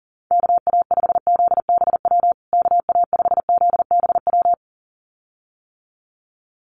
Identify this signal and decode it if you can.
Morse code, heard as a keyed tone, about 40 wpm, 700 Hz. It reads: KA5ZBW KA5ZBW
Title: USA Call Signs Repeated 2x 40wpm